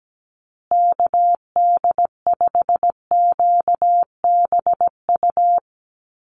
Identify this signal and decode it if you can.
Morse code, heard as a keyed tone, about 17 wpm, 700 Hz. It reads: KD5QBU
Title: USA Call Signs Repeated 3x 17wpm